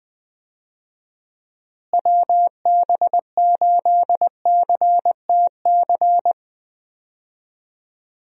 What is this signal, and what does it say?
Morse code, heard as a keyed tone, about 20 wpm, 700 Hz. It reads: WB8CTC
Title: USA Call Signs - 6 Characters - Repeated 3x - 20wpm